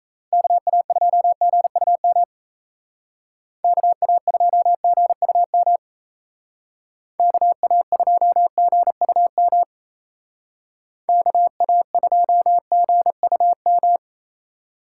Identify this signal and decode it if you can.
Morse code, heard as a keyed tone, about 36 wpm, 700 Hz. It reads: KA2GUM KA2GUM KA2GUM KA2GUM